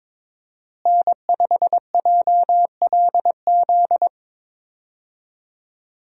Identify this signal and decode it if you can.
Morse code, heard as a keyed tone, about 21 wpm, 700 Hz. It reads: N5JLZ
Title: USA Call Signs - 5 Characters - Repeated 2x - 22wpm